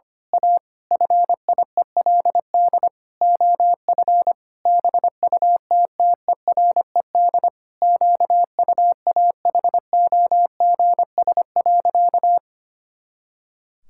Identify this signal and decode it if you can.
Morse code, heard as a keyed tone, about 25 wpm, 700 Hz. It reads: A FIELD OF BUTTERED QUAHOGS.
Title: Pangram Challenge v2 - 25wpm